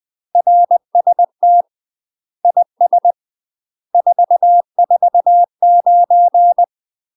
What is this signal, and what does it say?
Morse code, heard as a keyed tone, about 20 wpm, 700 Hz. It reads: RST IS 449
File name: UK MASTER QSOs TRAINING/20WPM